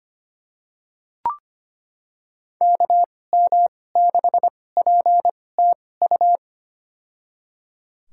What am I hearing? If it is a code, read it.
Morse code, heard as a keyed tone, about 25 wpm, spaced as for 18 wpm, 700 Hz. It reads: KM6PTU
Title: USA Call Signs - 2x Character Spacing - 25wpm